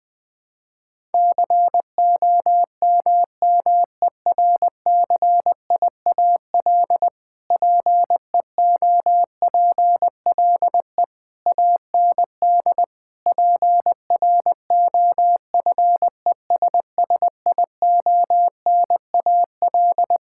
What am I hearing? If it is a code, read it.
Morse code, heard as a keyed tone, about 20 wpm, 700 Hz. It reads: COMMERCIAL PEOPLE AND PROFESSIONAL
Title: Sentences from Top 1000 Words - Review Part 5 - 20wpm